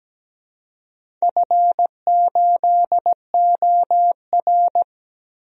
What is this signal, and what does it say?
Morse code, heard as a keyed tone, about 17 wpm, 700 Hz. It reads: F8OR
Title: DXCC-100 Call Signs - 4 Characters - 17wpm